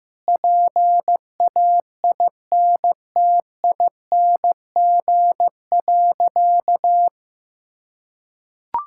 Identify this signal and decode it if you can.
Morse code, heard as a keyed tone, about 15 wpm, 700 Hz. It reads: PAINTING.
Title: Sentences from Top 1000 Words - Review Part 6 - 15wpm